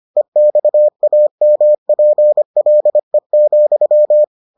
Morse code, heard as a keyed tone, about 25 wpm, 600 Hz. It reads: EXAMPLE,